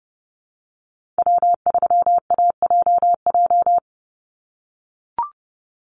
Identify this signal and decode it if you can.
Morse code, heard as a keyed tone, about 30 wpm, 700 Hz. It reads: W3AJJ
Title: USA Call Signs 30wpm